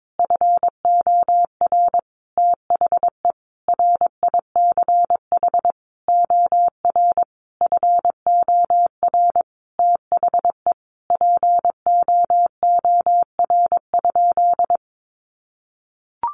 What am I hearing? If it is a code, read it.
Morse code, heard as a keyed tone, about 22 wpm, 700 Hz. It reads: FOR THE RICH OR FOR THE POOR?
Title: Sentences from Top 1100 Words 22wpm